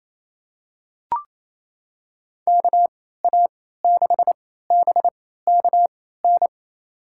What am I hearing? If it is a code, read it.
Morse code, heard as a keyed tone, about 28 wpm, spaced as for 16 wpm, 700 Hz. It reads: KA6BKN
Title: USA Call Signs - 6 Characters - 3x Character Spacing - 28wpm